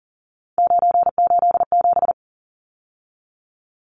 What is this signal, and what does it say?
Morse code, heard as a keyed tone, about 40 wpm, 700 Hz. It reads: 987
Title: Numbers - 3-Digit 40wpm